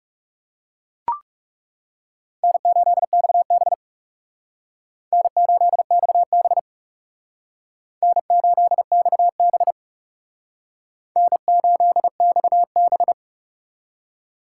Morse code, heard as a keyed tone, about 37 wpm, 700 Hz. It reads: N8XB N8XB N8XB N8XB
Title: Speed Racing - USA Call Signs 30wpm